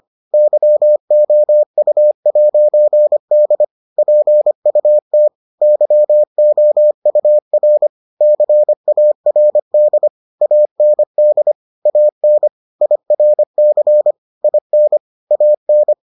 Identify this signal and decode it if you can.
Morse code, heard as a keyed tone, about 25 wpm, 600 Hz. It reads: YOU'D PUT YOUR CARD AND AN IRC IN AN